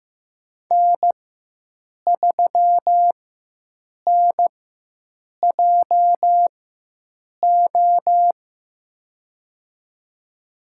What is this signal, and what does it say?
Morse code, heard as a keyed tone, about 15 wpm, spaced as for 7 wpm, 700 Hz. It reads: N3NJO
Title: USA Call Signs - 5 Characters - 4x Character Spacing - 15wpm